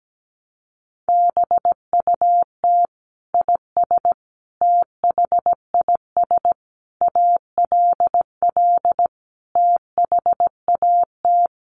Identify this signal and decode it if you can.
Morse code, heard as a keyed tone, about 17 wpm, 700 Hz. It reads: BUT IS THIS ALL THAT
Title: Sentences from Top 600 Words 17wpm